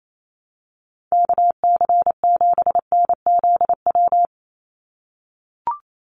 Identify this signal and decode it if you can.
Morse code, heard as a keyed tone, about 28 wpm, 700 Hz. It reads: KC7NZW
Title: USA Call Signs 28wpm